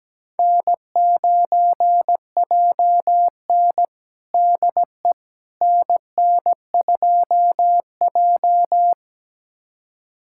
Morse code, heard as a keyed tone, about 17 wpm, 700 Hz. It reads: N9JN DE NN2J
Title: USA Callsign DE Callsign - 4 Characters - 17wpm